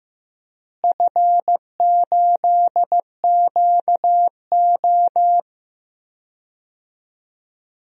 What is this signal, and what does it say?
Morse code, heard as a keyed tone, about 15 wpm, 700 Hz. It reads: F8QO
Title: DXCC-100 Call Signs - 4 Characters - Repeated 3x - 15wpm